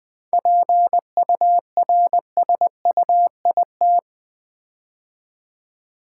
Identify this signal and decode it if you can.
Morse code, heard as a keyed tone, about 20 wpm, 700 Hz. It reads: PURSUIT
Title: 7 Letter Words - First 3 Letters Given - 20wpm